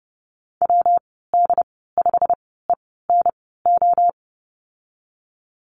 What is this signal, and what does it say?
Morse code, heard as a keyed tone, about 30 wpm, spaced as for 17 wpm, 700 Hz. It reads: WD5ENO
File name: Lesson 19g - Letter w - Call Signs 30wpm